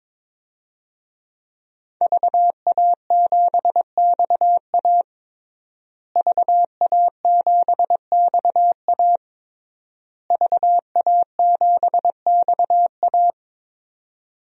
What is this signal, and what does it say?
Morse code, heard as a keyed tone, about 21 wpm, 700 Hz. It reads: VA7XA VA7XA VA7XA
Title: DXCC-100 Call Signs - 5 Characters - Repeated 3x - 22wpm